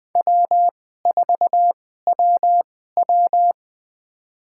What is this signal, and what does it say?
Morse code, heard as a keyed tone, about 21 wpm, spaced as for 15 wpm, 700 Hz. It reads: W4WW
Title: USA Call Signs - 4 Characters - 2x Character Spacing - 20wpm